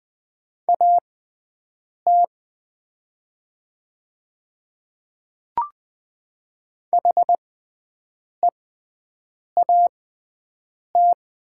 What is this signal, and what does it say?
Morse code, heard as a keyed tone, about 20 wpm, spaced as for 7 wpm, 700 Hz. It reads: AT HEAT
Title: 6x Character Spacing - Top 500 Words - 20wpm